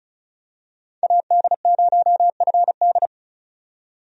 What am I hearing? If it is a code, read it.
Morse code, heard as a keyed tone, about 36 wpm, 700 Hz. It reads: AD0FD
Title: USA Call Signs - 5 Characters - 35wpm